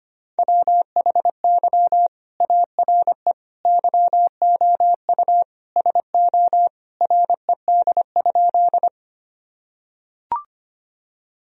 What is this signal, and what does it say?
Morse code, heard as a keyed tone, about 25 wpm, 700 Hz. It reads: WHY ARE YOU SO RED?
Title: Sentences from Top 1000 Words - Review Part 2 - 25wpm